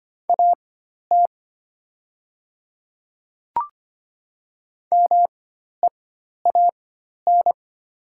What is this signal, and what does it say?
Morse code, heard as a keyed tone, about 25 wpm, spaced as for 12 wpm, 700 Hz. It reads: AT MEAN
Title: 4x Character Spacing - Top 500 Words - 25wpm